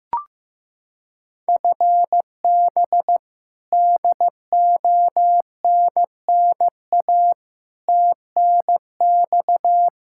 Morse code, heard as a keyed tone, about 15 wpm, 700 Hz. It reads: FB DONNA TNX
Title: Practice QSO - Exchange 3 - More Info - 15wpm